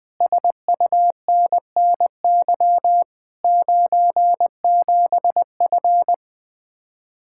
Morse code, heard as a keyed tone, about 21 wpm, 700 Hz. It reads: SUNNY 97F
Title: Speed-Racing Common QSO Phrases - 20wpm